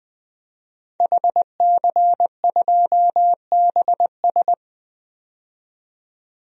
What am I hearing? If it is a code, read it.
Morse code, heard as a keyed tone, about 20 wpm, 700 Hz. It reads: HC2BS
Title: DXCC-100 Call Signs - 5 Characters - 20wpm